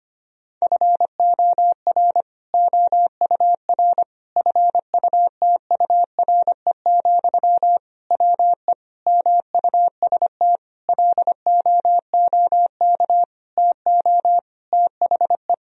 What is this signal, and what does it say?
Morse code, heard as a keyed tone, about 25 wpm, 700 Hz. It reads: FOR OUR FUTURE, WE MUST LOOK TO THE
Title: Sentences from Top 1000 Words - Review Part 3 - 25wpm